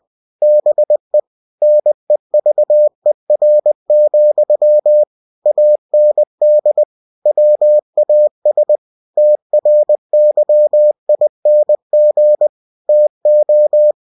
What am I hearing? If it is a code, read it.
Morse code, heard as a keyed tone, about 20 wpm, 600 Hz. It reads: BE NEVER, AND WAS TRYING TO
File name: cw-books / winnie-the-pooh / winnie-20wpm-page-0078